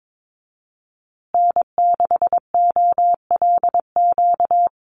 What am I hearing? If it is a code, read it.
Morse code, heard as a keyed tone, about 22 wpm, 700 Hz. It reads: N6OLQ
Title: USA Call Signs Repeated 3x 22wpm